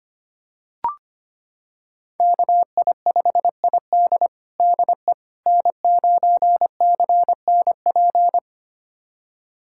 Morse code, heard as a keyed tone, about 25 wpm, 700 Hz. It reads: KI5ID DE N9CNP
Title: USA Callsign DE Callsign - 5 Characters - 25wpm